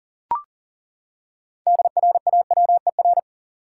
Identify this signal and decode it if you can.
Morse code, heard as a keyed tone, about 40 wpm, 700 Hz. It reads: DRAWER
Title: Six Letter Words - 40wpm